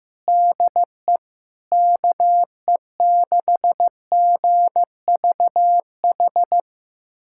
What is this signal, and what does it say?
Morse code, heard as a keyed tone, about 15 wpm, 700 Hz. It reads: DE KE6GVH
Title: USA Callsign DE Callsign - 6 Characters - 15wpm